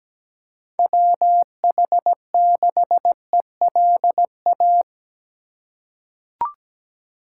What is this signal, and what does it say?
Morse code, heard as a keyed tone, about 17 wpm, 700 Hz. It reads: WH6ELA
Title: USA Call Signs - 6 Characters - 2x Character Spacing - 17wpm